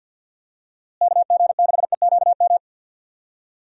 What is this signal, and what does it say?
Morse code, heard as a keyed tone, about 48 wpm, 700 Hz. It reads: KG6EQM
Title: USA Call Signs - 6 Characters - Repeated 3x - 50wpm